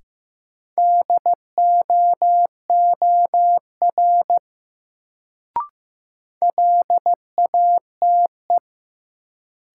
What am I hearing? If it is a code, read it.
Morse code, heard as a keyed tone, about 15 wpm, 700 Hz. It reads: DOOR LATE
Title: Sentences from Top 300 Words - New Word Drill - 15wpm